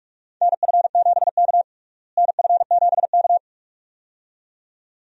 Morse code, heard as a keyed tone, about 46 wpm, 700 Hz. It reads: NF7K NF7K
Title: USA Call Signs - 4 Characters - Repeated 2x - 45wpm